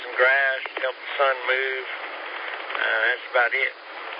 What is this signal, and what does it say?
Emergency Alert System (SAME) tones, heard as harsh buzzing bursts, with speech. It is heard from 0.2 to 3.2 s.